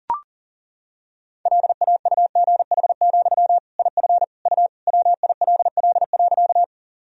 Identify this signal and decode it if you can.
Morse code, heard as a keyed tone, about 40 wpm, 700 Hz. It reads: LAUGH, IF U WILL.
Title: Sentences from Top 500 Words and Abbreviations 40wpm